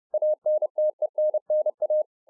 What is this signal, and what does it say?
Morse code, heard as a keyed tone, about 30 wpm, 610 Hz. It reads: ANTENNA